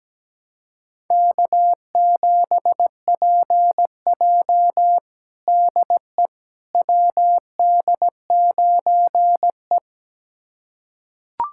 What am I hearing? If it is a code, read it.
Morse code, heard as a keyed tone, about 17 wpm, 700 Hz. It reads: K7PJ DE WD9E
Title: USA Callsign DE Callsign - 4 Characters - 17wpm